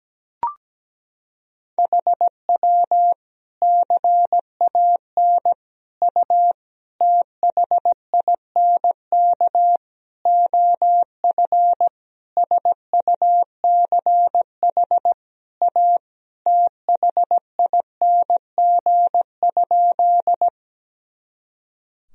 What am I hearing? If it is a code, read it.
Morse code, heard as a keyed tone, about 17 wpm, 700 Hz. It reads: HW CAN U THINK OF SUCH A THING?
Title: Sentences from Top 200 Words and Abbreviations 17wpm